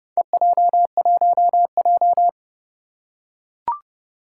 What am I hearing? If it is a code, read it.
Morse code, heard as a keyed tone, about 30 wpm, 700 Hz. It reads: EJ1J
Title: Four Challenging Letter-Number Combinations 30wpm